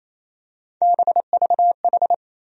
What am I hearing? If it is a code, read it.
Morse code, heard as a keyed tone, about 28 wpm, 700 Hz. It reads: BVH
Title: USA Call Sign Postfixes - 3 Letters 28wpm